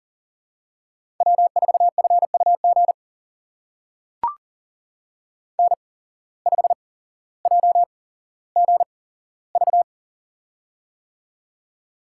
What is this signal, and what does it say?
Morse code, heard as a keyed tone, about 39 wpm, spaced as for 17 wpm, 700 Hz. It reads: W4FUG N5JGV
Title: USA Call Signs - 5 Characters - 8x Character Spacing - 40wpm